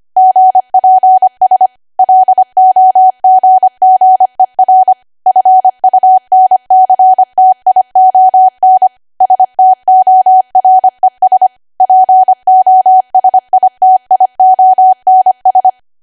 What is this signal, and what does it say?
Morse code, heard as a keyed tone, about 25 wpm, 750 Hz. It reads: GPS LOGGER FUNCTION STORES POSITIONS
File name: Archive / 25 WPM / 25WPM